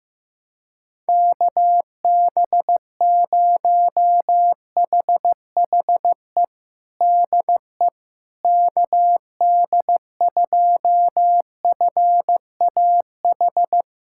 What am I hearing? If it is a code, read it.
Morse code, heard as a keyed tone, about 15 wpm, 700 Hz. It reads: KB0HHE DE KD2FAH
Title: USA Callsign DE Callsign 15wpm